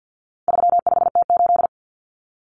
Morse code, heard as a keyed tone, about 50 wpm, 700 Hz. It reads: 35T8
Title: Common Cut Numbers - 4-Digit 50wpm